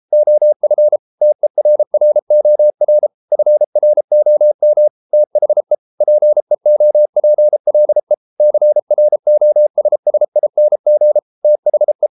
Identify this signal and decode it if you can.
Morse code, heard as a keyed tone, about 33 wpm, 600 Hz. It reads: OF TERROR FROM THE PEOPLE CROSSING THE